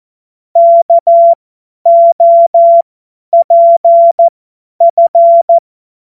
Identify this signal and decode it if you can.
Morse code, heard as a keyed tone, about 14 wpm, spaced as for 10 wpm, 680 Hz. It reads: KOPF
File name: Morsekurs / Witze / Witz 17.Oktober 2023 50 F70